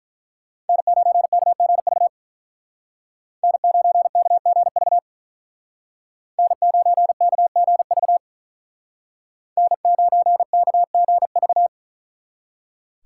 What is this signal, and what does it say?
Morse code, heard as a keyed tone, about 43 wpm, 700 Hz. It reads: N9KGV N9KGV N9KGV N9KGV
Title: Speed Racing - USA 1x3 Call Signs 35wpm